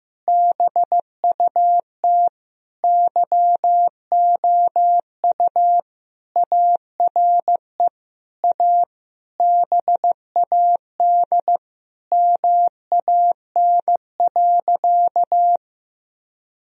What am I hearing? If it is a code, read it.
Morse code, heard as a keyed tone, about 15 wpm, 700 Hz. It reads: BUT YOU ARE A BAD MAN.
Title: Sentences from Top 1000 Words - Review Part 1 - 15wpm